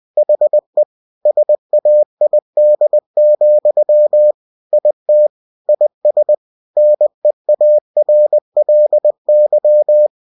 Morse code, heard as a keyed tone, about 20 wpm, 600 Hz. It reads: HE SAID, IT IS NEARLY
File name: cw-books / winnie-the-pooh / winnie-20wpm-page-0025